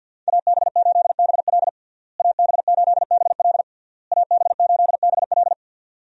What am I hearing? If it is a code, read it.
Morse code, heard as a keyed tone, about 51 wpm, 700 Hz. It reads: AB8BL AB8BL AB8BL
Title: USA Call Signs - 5 Characters - Repeated 3x - 50wpm